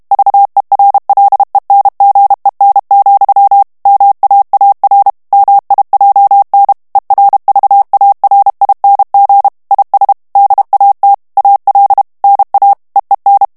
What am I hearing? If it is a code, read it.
Morse code, heard as a keyed tone, about 32 wpm, 800 Hz. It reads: VERLENGEN, MAAR MIJN ERVARING IS DAT AL NA EEN
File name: HOLLANDAIS texte clair 2 vitesse 30 mots minute